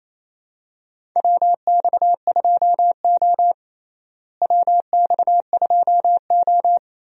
Morse code, heard as a keyed tone, about 28 wpm, 700 Hz. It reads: WX2O WX2O
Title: USA Call Signs - 4 Characters - Repeated 2x - 28wpm